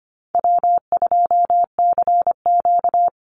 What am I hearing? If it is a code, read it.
Morse code, heard as a keyed tone, about 25 wpm, 700 Hz. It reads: W2CQ
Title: Easy Call Signs Repeated 2x - 25wpm